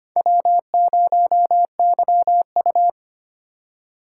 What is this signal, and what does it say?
Morse code, heard as a keyed tone, about 25 wpm, 700 Hz. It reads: W0YU
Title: USA Call Signs - 4 Characters - 2x Character Spacing - 25wpm